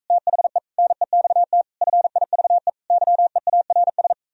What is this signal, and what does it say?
Morse code, heard as a keyed tone, about 45 wpm, 700 Hz. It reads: THE NEXT FIVE YEARS